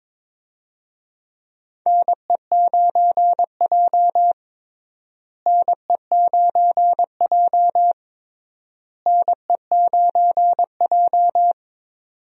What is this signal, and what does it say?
Morse code, heard as a keyed tone, about 22 wpm, 700 Hz. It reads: NE9J NE9J NE9J
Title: USA Call Signs - 4 Characters - Repeated 3x - 22wpm